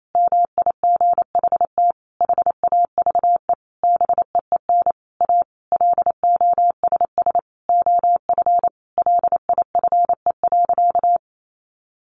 Morse code, heard as a keyed tone, about 28 wpm, 700 Hz. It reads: MIGHT HAVE BEEN A LOSS OF LIFE.
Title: Sentences from Top 1400 Words 28wpm